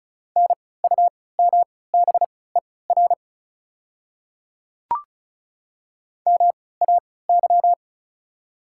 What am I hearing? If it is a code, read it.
Morse code, heard as a keyed tone, about 36 wpm, spaced as for 20 wpm, 690 Hz. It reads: NUMBER MAY
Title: 3x Character Spacing - Top 100 Words - 35wpm